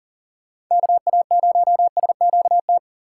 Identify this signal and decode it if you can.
Morse code, heard as a keyed tone, about 38 wpm, 700 Hz. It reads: KA0SQT
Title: USA Call Signs - 6 Characters - 3x Character Spacing - 40wpm